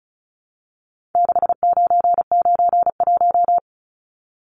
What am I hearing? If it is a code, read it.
Morse code, heard as a keyed tone, about 34 wpm, 700 Hz. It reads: 6991
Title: Numbers - 4-Digit 35wpm